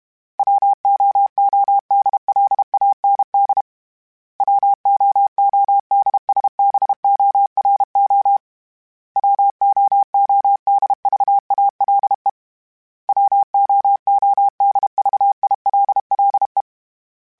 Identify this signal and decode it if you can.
Morse code, heard as a keyed tone, about 32 wpm, 800 Hz. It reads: WOODLAND WOODSBORO WOODVALE WOODVILLE
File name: fromSMs / FromSMs mp3s / Cities@30